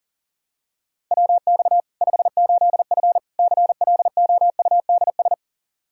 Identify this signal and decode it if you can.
Morse code, heard as a keyed tone, about 40 wpm, 700 Hz. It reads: WX 58F CLOUDS
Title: Practice QSO - Exchange 2 - Info - 40wpm